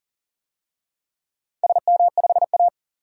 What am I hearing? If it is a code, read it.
Morse code, heard as a keyed tone, about 44 wpm, 700 Hz. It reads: SM5A
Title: DXCC-100 Call Signs - 4 Characters - 40wpm